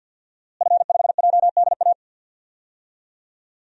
Morse code, heard as a keyed tone, about 49 wpm, 700 Hz. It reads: F5JDA
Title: DXCC-100 Call Signs - 5 Characters - Repeated 3x - 50wpm